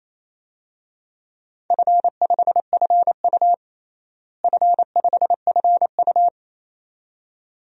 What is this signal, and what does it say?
Morse code, heard as a keyed tone, about 28 wpm, 700 Hz. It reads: F5FU F5FU
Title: DXCC-100 Call Signs - 4 Characters - Repeated 2x - 28wpm